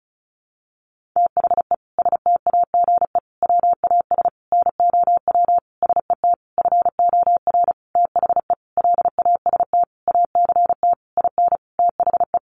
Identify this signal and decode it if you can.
Morse code, heard as a keyed tone, about 34 wpm, 700 Hz. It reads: THE STAGE WAS NOW SET FOR THE LAST ACT IN THE
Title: Sentences from Top 1500 Words 35wpm